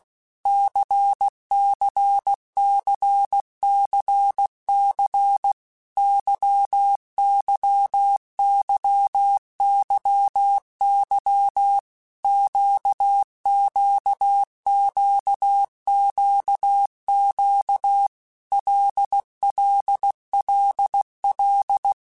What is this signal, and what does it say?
Morse code, heard as a keyed tone, about 16 wpm, 800 Hz. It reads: CCCCC YYYYY QQQQQ LLLL
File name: Basic Letter Groups By7Groups / LFCYQ / LFCYQ@15